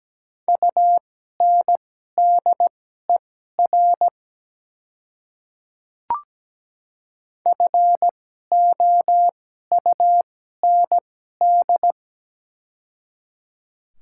Morse code, heard as a keyed tone, about 17 wpm, spaced as for 12 wpm, 700 Hz. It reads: UNDER FOUND
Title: 2x Character Spacing - Top 300 Words - 17wpm